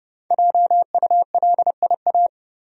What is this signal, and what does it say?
Morse code, heard as a keyed tone, about 30 wpm, 700 Hz. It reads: JULIA
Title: USA Common Names - 30wpm